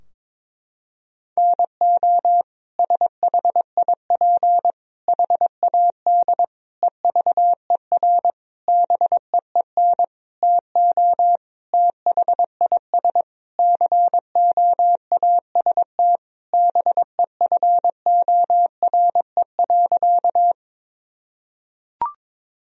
Morse code, heard as a keyed tone, about 22 wpm, 700 Hz. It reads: NO SHIP HAD EVER BEEN TO THIS COAST BEFORE.
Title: Sentences from Top 1000 Words - Review Part 4 - 22wpm